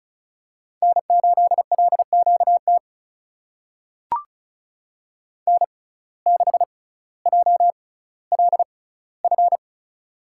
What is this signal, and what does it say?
Morse code, heard as a keyed tone, about 36 wpm, spaced as for 18 wpm, 700 Hz. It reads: N8LQT N6JLF
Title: USA Call Signs - 5 Characters - 6x Character Spacing - 35wpm